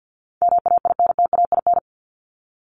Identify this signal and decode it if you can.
Morse code, heard as a keyed tone, about 50 wpm, 700 Hz. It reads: MAINTAIN
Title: Top 501-750 Words 50wpm